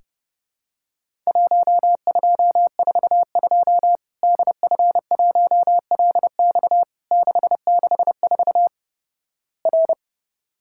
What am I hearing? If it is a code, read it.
Morse code, heard as a keyed tone, about 32 wpm, 700 Hz. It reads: 1242 DF1LX 664 R
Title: Worked All Europe DX Contest - QTC Exchange - 30wpm